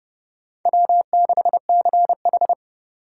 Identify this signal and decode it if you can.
Morse code, heard as a keyed tone, about 29 wpm, 700 Hz. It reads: W6CH
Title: USA Call Signs - 3x Character Spacing - 30wpm